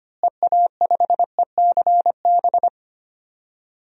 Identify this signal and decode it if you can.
Morse code, heard as a keyed tone, about 25 wpm, 700 Hz. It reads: EA5ECB
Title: DXCC-100 Call Signs - 6 Characters - Repeated 3x - 25wpm